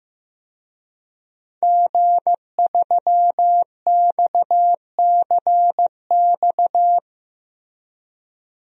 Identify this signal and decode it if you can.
Morse code, heard as a keyed tone, about 14 wpm, 700 Hz. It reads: G3XCX
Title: DXCC-100 Call Signs - 5 Characters - Repeated 3x - 15wpm